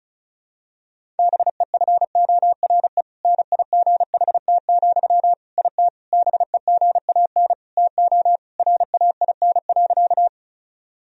Morse code, heard as a keyed tone, about 34 wpm, 700 Hz. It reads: BEFORE NIGHT, IT BEGAN TO RAIN.
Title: Sentences from Top 1000 Words - Review Part 1 - 35wpm